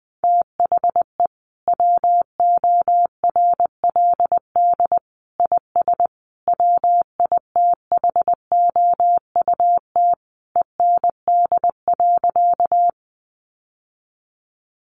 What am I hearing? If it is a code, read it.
Morse code, heard as a keyed tone, about 20 wpm, 700 Hz. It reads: THE WORLD IS WITHOUT END.
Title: Sentences from Top 600 Words 20wpm